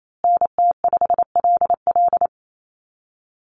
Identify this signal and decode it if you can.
Morse code, heard as a keyed tone, about 28 wpm, 700 Hz. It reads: NT5LL
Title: USA Call Signs Repeated 3x 28wpm